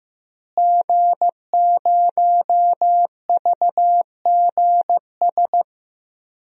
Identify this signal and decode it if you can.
Morse code, heard as a keyed tone, about 15 wpm, 700 Hz. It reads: G0VGS
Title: DXCC-100 Call Signs - 5 Characters - Repeated 3x - 15wpm